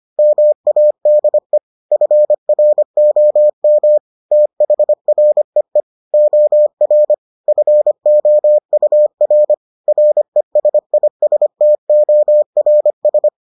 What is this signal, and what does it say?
Morse code, heard as a keyed tone, about 25 wpm, 600 Hz. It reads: MADE FROM THREE OR FOUR RESISTORS